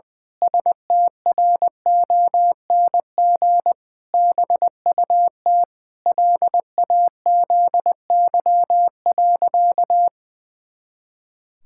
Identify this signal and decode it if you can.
Morse code, heard as a keyed tone, about 20 wpm, 700 Hz. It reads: STRONG BUT LAZY.
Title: Pangram Challenge v2 - 20wpm